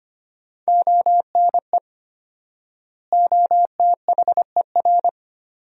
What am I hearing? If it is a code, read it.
Morse code, heard as a keyed tone, about 25 wpm, 700 Hz. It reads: ONE OTHER
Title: 4x Word-Spacing - Sentences from Top 400 Words - 25wpm